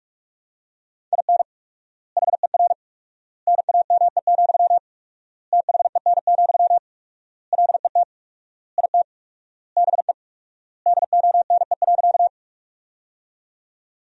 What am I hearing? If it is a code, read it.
Morse code, heard as a keyed tone, about 46 wpm, 700 Hz. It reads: IN HER NAME, THEN, LET IT BE DONE.
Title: 4x Word-Spacing - Sentences from Top 500 Words - 45wpm